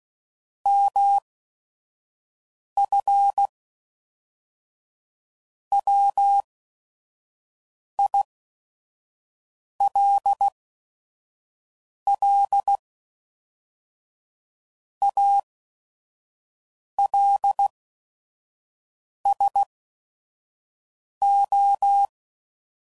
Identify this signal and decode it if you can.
Morse code, heard as a keyed tone, about 16 wpm, 800 Hz. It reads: M F W I L L A L S O